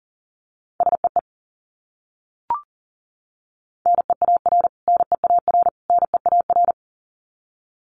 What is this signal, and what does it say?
Morse code, heard as a keyed tone, about 40 wpm, 700 Hz. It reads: SEE NEAR NEAR NEAR
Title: Top 300 Words Repeated 40wpm